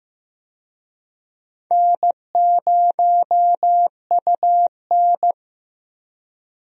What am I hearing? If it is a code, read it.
Morse code, heard as a keyed tone, about 15 wpm, 700 Hz. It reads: N0UN
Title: USA Call Signs - 4 Characters - 15wpm